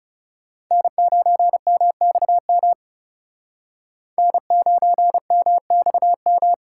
Morse code, heard as a keyed tone, about 37 wpm, 700 Hz. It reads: N9MXM N9MXM
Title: Speed Racing - USA 1x3 Call Signs 30wpm